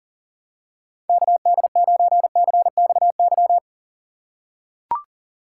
Morse code, heard as a keyed tone, about 40 wpm, 700 Hz. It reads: KD9CXY
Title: USA Call Signs - 6 Characters - 2x Character Spacing - 40wpm